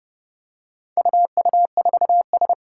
Morse code, heard as a keyed tone, about 30 wpm, 700 Hz. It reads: UU4S
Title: Four Challenging Letter-Number Combinations 30wpm